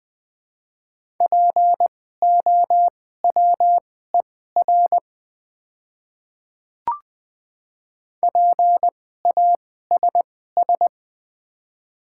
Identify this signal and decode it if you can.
Morse code, heard as a keyed tone, about 21 wpm, spaced as for 15 wpm, 700 Hz. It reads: POWER PASS
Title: 2x Character Spacing - Top 500 Words - 20wpm